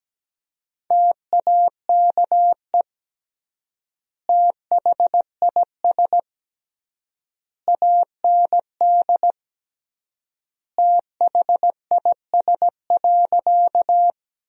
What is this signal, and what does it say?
Morse code, heard as a keyed tone, about 17 wpm, 700 Hz. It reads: TAKE THIS AND THIS.
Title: 3x Word-Spacing - Sentences from Top 200 Words - 17wpm